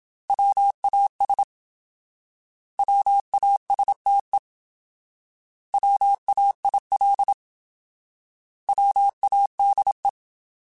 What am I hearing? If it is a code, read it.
Morse code, heard as a keyed tone, about 26 wpm, 800 Hz. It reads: WAS WASTE WAIL WADE
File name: Alphabetical / Alphabetical mp3s / Ws @25